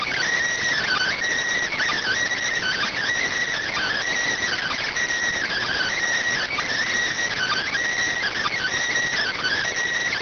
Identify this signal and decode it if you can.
Emergency Alert System (SAME) tones (header bursts), heard as harsh buzzing bursts. It is heard from 0.8 to 9.6 s.